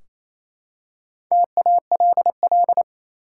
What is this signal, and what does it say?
Morse code, heard as a keyed tone, about 27 wpm, 700 Hz. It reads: TALL
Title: Four Letter Words - 28wpm